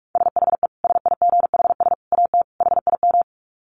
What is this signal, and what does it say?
Morse code, heard as a keyed tone, about 44 wpm, 700 Hz. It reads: SHE SIGHS AT HIM
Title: Dit Challenge 45wpm